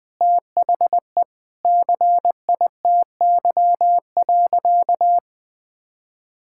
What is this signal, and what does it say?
Morse code, heard as a keyed tone, about 20 wpm, 700 Hz. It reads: THE CITY.
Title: Sentences with 4-Word Phrases - Spoken Beginning - 20wpm